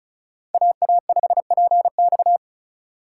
Morse code, heard as a keyed tone, about 44 wpm, 700 Hz. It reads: AA5PX
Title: Speed Racing - USA 2x2 Call Signs 35wpm